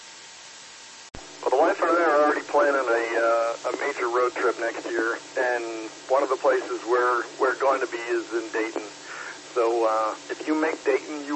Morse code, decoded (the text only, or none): none